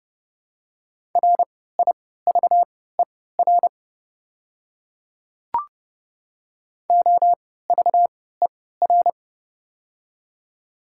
Morse code, heard as a keyed tone, about 30 wpm, spaced as for 17 wpm, 700 Hz. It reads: RIVER OVER
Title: 3x Character Spacing - Top 500 Words - 30wpm